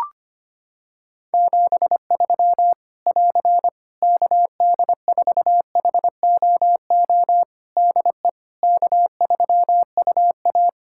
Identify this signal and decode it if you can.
Morse code, heard as a keyed tone, about 25 wpm, 700 Hz. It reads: 73 <AR> KD4HOO DE K3UA
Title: Practice QSO - Exchange 4 - Ending - 25wpm